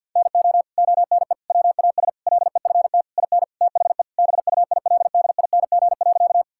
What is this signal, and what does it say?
Morse code, heard as a keyed tone, about 50 wpm, 700 Hz. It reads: NO ONE WAS LEFT IN THE BUILDING.
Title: Sentences from Top 1000 Words - Review Part 5 - 50wpm